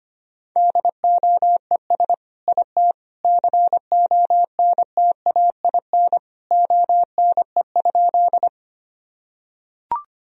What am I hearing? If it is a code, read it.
Morse code, heard as a keyed tone, about 25 wpm, 700 Hz. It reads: DOES IT CONTAIN ONE?
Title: Sentences from Top 1000 Words - Review Part 2 - 25wpm